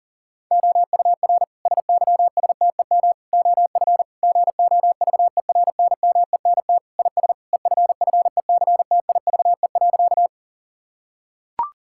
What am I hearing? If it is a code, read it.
Morse code, heard as a keyed tone, about 40 wpm, 700 Hz. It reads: OUR SYSTEM OF GOVERNMENT IS EFFECTIVE.
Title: Sentences from Top 1000 Words - Review Part 6 - 40wpm